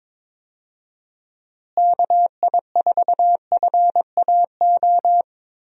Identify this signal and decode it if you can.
Morse code, heard as a keyed tone, about 21 wpm, 700 Hz. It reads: KI4FAO
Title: USA Call Signs - 6 Characters - Repeated 3x - 22wpm